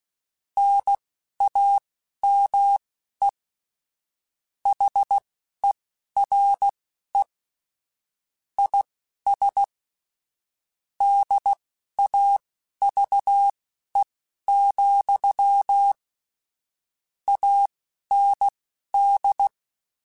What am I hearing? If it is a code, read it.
Morse code, heard as a keyed tone, about 16 wpm, spaced as for 11 wpm, 800 Hz. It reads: NAME HERE IS DAVE, AND
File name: QSOs / QSOs-mp3s / kc7sac@10